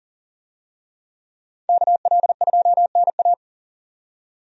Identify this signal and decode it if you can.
Morse code, heard as a keyed tone, about 40 wpm, 700 Hz. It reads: KL2NA
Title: USA Call Signs - 5 Characters - 40wpm